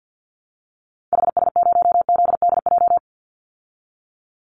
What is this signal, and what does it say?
Morse code, heard as a keyed tone, about 48 wpm, 700 Hz. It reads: HS0ZDJ
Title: Call Signs Repeated 3x - 50wpm